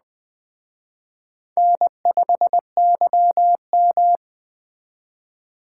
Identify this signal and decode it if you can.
Morse code, heard as a keyed tone, about 19 wpm, 700 Hz. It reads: N5YM